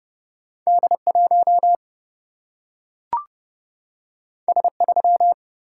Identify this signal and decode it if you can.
Morse code, heard as a keyed tone, about 30 wpm, 700 Hz. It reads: D 1 S 3
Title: Single-Letter Single-Number 30wpm